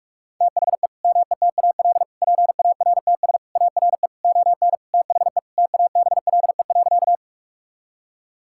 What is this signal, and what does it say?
Morse code, heard as a keyed tone, about 46 wpm, 700 Hz. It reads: THE METAL PARTS ARE ON THE TABLE.
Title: Sentences from Top 1000 Words - Review Part 6 - 45wpm